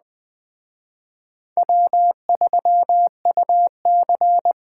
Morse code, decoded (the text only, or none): W3UC